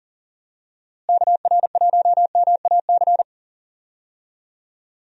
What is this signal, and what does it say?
Morse code, heard as a keyed tone, about 37 wpm, 700 Hz. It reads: KR1MAC
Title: USA Call Signs - 6 Characters - Repeated 3x - 40wpm